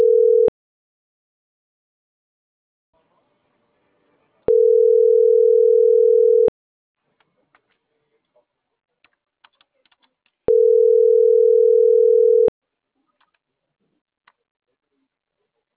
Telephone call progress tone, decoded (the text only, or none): ringback tone